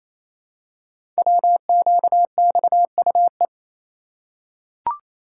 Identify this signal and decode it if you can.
Morse code, heard as a keyed tone, about 29 wpm, 700 Hz. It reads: WQXUE
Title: Code Groups - 5 Letters - 28wpm